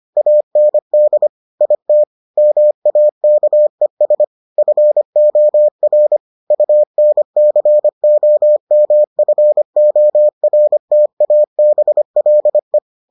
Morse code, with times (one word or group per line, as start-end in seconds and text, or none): 0.2-1.3 AND
1.6-2.0 IT
2.4-4.2 MAKES
4.6-6.2 FOR
6.5-12.8 UNCOMFORTABLE